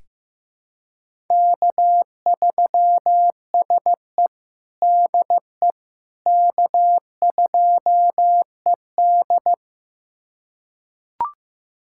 Morse code, heard as a keyed tone, about 15 wpm, 700 Hz. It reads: K3SE DE K2ED
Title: USA Callsign DE Callsign - 4 Characters - 15wpm